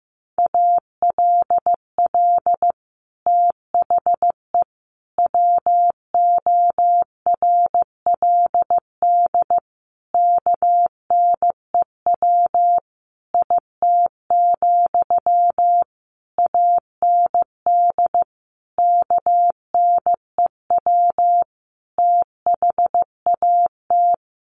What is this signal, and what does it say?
Morse code, heard as a keyed tone, about 15 wpm, 700 Hz. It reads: ALL THE WORLD KNEW IT, AND KNEW THAT
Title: Sentences from Top 1200 Words 15wpm